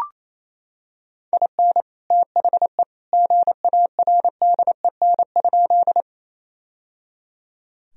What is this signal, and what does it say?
Morse code, heard as a keyed tone, about 27 wpm, 700 Hz. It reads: IN THE GARDEN?
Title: Sentences from Top 1000 Words - Review Part 4 - 28wpm